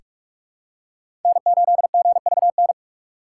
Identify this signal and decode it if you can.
Morse code, heard as a keyed tone, about 46 wpm, 700 Hz. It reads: N8GVN
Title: USA Call Signs - 5 Characters - 4x Character Spacing - 45wpm